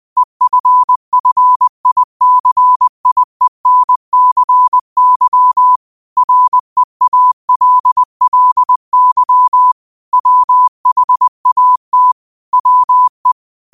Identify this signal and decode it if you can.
Morse code, heard as a keyed tone, about 20 wpm, 1000 Hz. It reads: EFFICIENCY REALLY WHAT WE
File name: qotdfiles / 20wpm